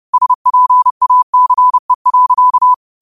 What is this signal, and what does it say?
Morse code, heard as a keyed tone, about 30 wpm, 1000 Hz. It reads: SPACE.